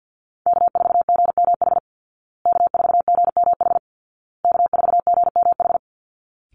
Morse code, heard as a keyed tone, about 50 wpm, 700 Hz. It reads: K4GMH K4GMH K4GMH
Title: Call Signs Repeated 2x - 50wpm